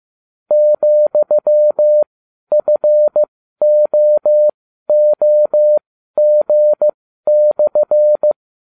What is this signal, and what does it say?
Morse code, heard as a keyed tone, about 15 wpm, spaced as for 12 wpm, 600 Hz. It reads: ,FOOG/